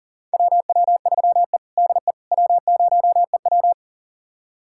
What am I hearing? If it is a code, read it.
Morse code, heard as a keyed tone, about 38 wpm, 700 Hz. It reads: WW3E DE W0EW
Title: USA Callsign DE Callsign - 4 Characters - 40wpm